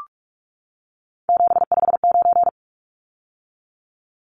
Morse code, heard as a keyed tone, about 44 wpm, 700 Hz. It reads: 759
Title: Numbers - 3-Digit 45wpm